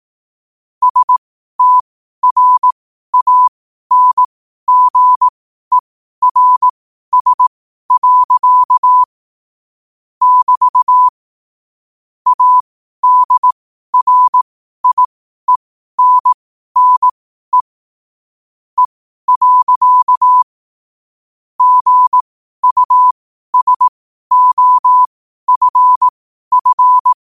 Morse code, heard as a keyed tone, about 18 wpm, spaced as for 12 wpm, 1000 Hz. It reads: STRANGERS. <BT> ADRIENNE E. GUSOFF
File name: qotdfiles / 12wpm